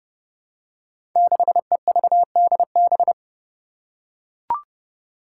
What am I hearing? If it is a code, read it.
Morse code, heard as a keyed tone, about 30 wpm, 700 Hz. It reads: 6EVDB
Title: Five Challenging Letter-Number Combinations 30wpm